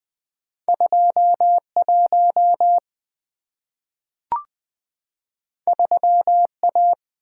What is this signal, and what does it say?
Morse code, heard as a keyed tone, about 20 wpm, 700 Hz. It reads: 21 3A
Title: Two Letter-Number Combinations - 20wpm